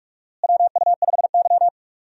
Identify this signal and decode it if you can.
Morse code, heard as a keyed tone, about 42 wpm, 700 Hz. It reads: WU5Y
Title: USA Call Signs - 4 Characters - Repeated 3x - 45wpm